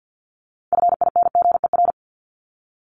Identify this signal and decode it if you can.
Morse code, heard as a keyed tone, about 50 wpm, 700 Hz. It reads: FINGER
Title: Top 1501-1750 Words 50wpm